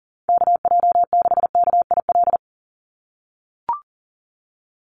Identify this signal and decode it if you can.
Morse code, heard as a keyed tone, about 42 wpm, 700 Hz. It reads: KJ6KIL
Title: USA Call Signs 40wpm